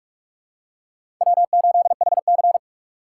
Morse code, heard as a keyed tone, about 46 wpm, 700 Hz. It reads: W8HC
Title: USA Call Signs - 4 Characters - 3x Character Spacing - 45wpm